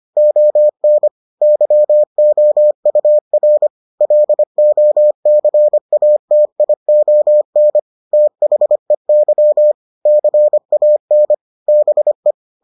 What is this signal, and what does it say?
Morse code, heard as a keyed tone, about 25 wpm, 600 Hz. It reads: ON YOUR LOCATION THEY CAN BE